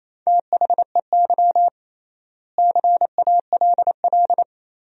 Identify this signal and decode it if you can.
Morse code, heard as a keyed tone, about 28 wpm, 700 Hz. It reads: THEY CALL
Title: 3x Word-Spacing - Sentences from Top 500 Words - 28wpm